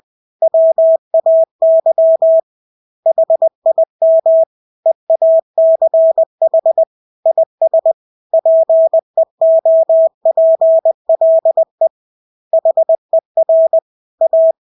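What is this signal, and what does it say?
Morse code, heard as a keyed tone, about 20 wpm, 650 Hz. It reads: WAY HIM EACH IS PEOPLE HER A
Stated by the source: Morse practice of English words at 20 wpm.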